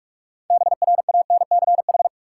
Morse code, heard as a keyed tone, about 46 wpm, 700 Hz. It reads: BRANCH
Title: Six Letter Words - 45wpm